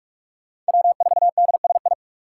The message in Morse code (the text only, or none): W4DSI